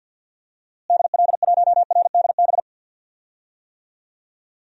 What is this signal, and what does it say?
Morse code, heard as a keyed tone, about 52 wpm, 700 Hz. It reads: DL1RDB
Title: DXCC-100 Call Signs - 6 Characters - Repeated 3x - 50wpm